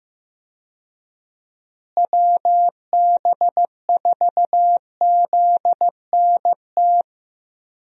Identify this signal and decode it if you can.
Morse code, heard as a keyed tone, about 15 wpm, 700 Hz. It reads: WB4ZNT
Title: USA Call Signs - 6 Characters - Repeated 2x - 15wpm